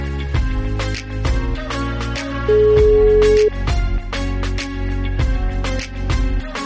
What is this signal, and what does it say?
Ringback tone (Europe, 425 Hz), with noise.